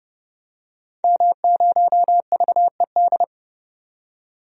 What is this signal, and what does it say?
Morse code, heard as a keyed tone, about 32 wpm, 700 Hz. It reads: M0VED
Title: DXCC-100 Call Signs - 5 Characters - 30wpm